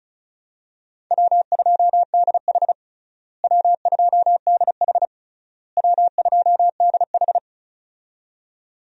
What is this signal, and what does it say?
Morse code, heard as a keyed tone, about 34 wpm, 700 Hz. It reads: W2DH W2DH W2DH
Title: USA Call Signs - 4 Characters - Repeated 3x - 35wpm